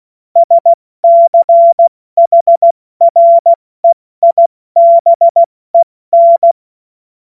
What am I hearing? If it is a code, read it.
Morse code, heard as a keyed tone, about 16 wpm, 680 Hz. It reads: SCHREIBEN
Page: cw-witze